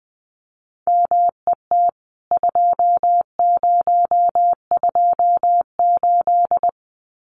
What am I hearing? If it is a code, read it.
Morse code, heard as a keyed tone, about 20 wpm, 700 Hz. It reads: MET 2028
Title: CWT Exchange 20wpm